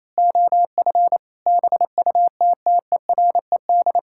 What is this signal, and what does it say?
Morse code, heard as a keyed tone, about 27 wpm, 700 Hz. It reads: OF BUTTERED
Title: Pangram Challenge v2 - 28wpm